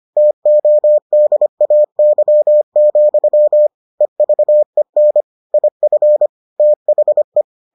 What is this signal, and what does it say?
Morse code, heard as a keyed tone, about 25 wpm, 600 Hz. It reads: TODAY, EVEN IF THE